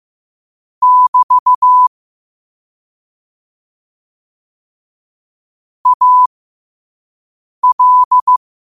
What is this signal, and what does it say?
Morse code, heard as a keyed tone, about 15 wpm, spaced as for 5 wpm, 1000 Hz. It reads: <BT> AL
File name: qotdfiles / 5wpm